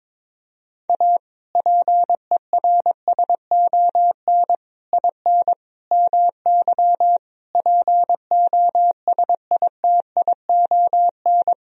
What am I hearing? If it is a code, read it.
Morse code, heard as a keyed tone, about 22 wpm, 700 Hz. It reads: A PERSON IN MY POSITION
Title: Sentences from Top 1000 Words - Review Part 4 - 22wpm